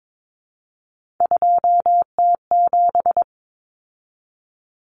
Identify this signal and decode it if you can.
Morse code, heard as a keyed tone, about 22 wpm, 700 Hz. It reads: 2T7
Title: Common Cut Numbers - 3-Digit 22wpm